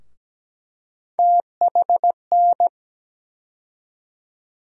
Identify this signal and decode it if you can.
Morse code, heard as a keyed tone, about 17 wpm, 700 Hz. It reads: THN
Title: USA Call Sign Postfixes - 3 Letters 17wpm